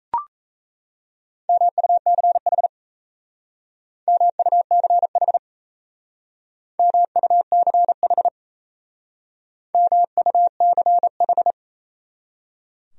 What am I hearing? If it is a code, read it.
Morse code, heard as a keyed tone, about 34 wpm, 700 Hz. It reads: MUCH MUCH MUCH MUCH
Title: Speed-Racing Top 200 Words 28wpm